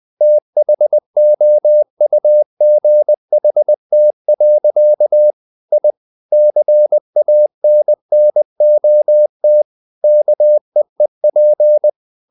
Morse code, heard as a keyed tone, about 20 wpm, 600 Hz. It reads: THOUGHT. I CANNOT KEEP
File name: the old man and the sea part 2 20wpm